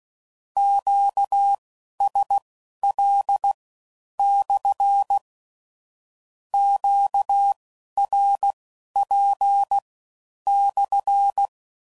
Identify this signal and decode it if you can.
Morse code, heard as a keyed tone, about 16 wpm, spaced as for 11 wpm, 800 Hz. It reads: QSL/ QRP/
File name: CW Code Groups @15WPM / Q Calls / Q Calls SPSL1